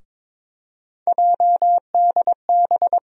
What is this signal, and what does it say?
Morse code, heard as a keyed tone, about 22 wpm, 700 Hz. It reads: JDB
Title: USA Call Sign Postfixes - 3 Letters 22wpm